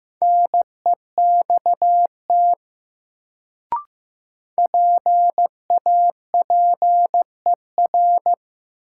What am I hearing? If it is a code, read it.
Morse code, heard as a keyed tone, about 15 wpm, 700 Hz. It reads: NEXT PAPER
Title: Sentences from Top 300 Words - New Word Drill - 15wpm